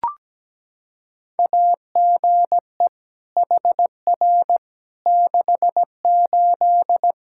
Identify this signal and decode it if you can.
Morse code, heard as a keyed tone, about 17 wpm, 700 Hz. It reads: AGE HR 68
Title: K1USN SST QSO Practice 17wpm